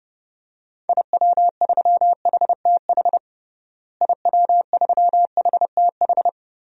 Morse code, heard as a keyed tone, about 30 wpm, 700 Hz. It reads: IW3HTH IW3HTH
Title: DXCC-100 Call Signs - 6 Characters - Repeated 3x - 30wpm